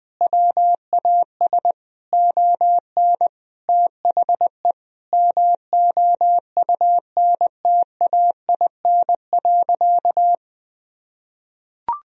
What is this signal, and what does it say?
Morse code, heard as a keyed tone, about 21 wpm, 700 Hz. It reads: WAS ON THE MOUNTAIN.
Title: Sentences from Top 1000 Words - Review Part 1 - 20wpm